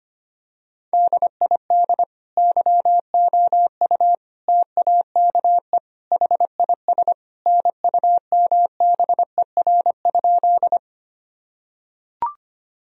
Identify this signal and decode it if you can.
Morse code, heard as a keyed tone, about 25 wpm, 700 Hz. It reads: DID YOU TAKE HIS NUMBER?
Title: Sentences from Top 1000 Words - Review Part 1 - 25wpm